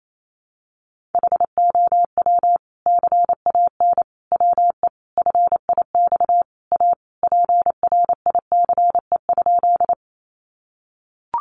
Morse code, heard as a keyed tone, about 28 wpm, 700 Hz. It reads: HOW CAN WE FIX A PRICE?
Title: Sentences from Top 1400 Words 28wpm